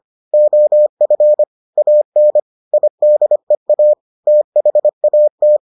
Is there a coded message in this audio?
Yes: Morse code from 0.3 to 5.6 s, heard as a keyed tone, about 25 wpm, 600 Hz. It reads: OF AN IDEA THAT